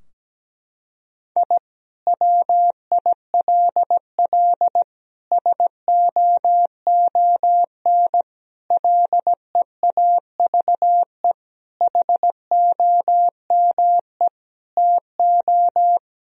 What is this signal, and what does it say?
Morse code, heard as a keyed tone, about 17 wpm, 700 Hz. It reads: I WILL SOON LEAVE HOME TO
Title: Sentences from Top 1000 Words - Review Part 2 - 17wpm